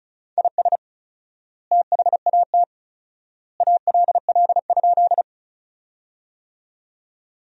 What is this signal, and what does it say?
Morse code, heard as a keyed tone, about 34 wpm, 700 Hz. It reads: IS THAT ALL?
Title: 4x Word-Spacing - Sentences from Top 100 Words - 35wpm